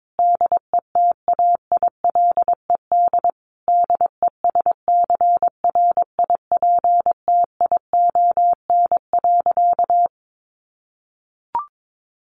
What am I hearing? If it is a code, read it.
Morse code, heard as a keyed tone, about 22 wpm, 700 Hz. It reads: DETAILED DESCRIPTION.
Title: Sentences from Top 1500 Words 22wpm